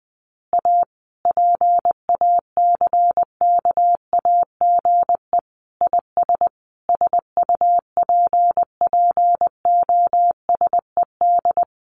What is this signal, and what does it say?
Morse code, heard as a keyed tone, about 20 wpm, 700 Hz. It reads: A PACKAGE IS SUPPOSED
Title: Easy Phrases 20wpm